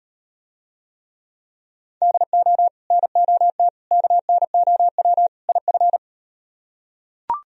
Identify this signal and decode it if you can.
Morse code, heard as a keyed tone, about 38 wpm, 700 Hz. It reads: DO NOT KNOW IF